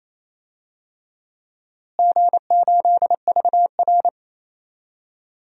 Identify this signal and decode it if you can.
Morse code, heard as a keyed tone, about 28 wpm, 700 Hz. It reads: G8VR
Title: DXCC-100 Call Signs - 4 Characters - Repeated 2x - 28wpm